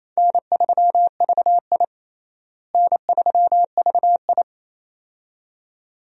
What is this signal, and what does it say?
Morse code, heard as a keyed tone, about 28 wpm, 700 Hz. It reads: N3VI N3VI
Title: USA Call Signs - 4 Characters - Repeated 2x - 28wpm